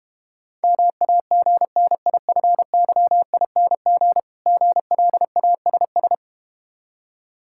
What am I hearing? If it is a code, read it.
Morse code, heard as a keyed tone, about 32 wpm, 700 Hz. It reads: MAGNIFYING GLASS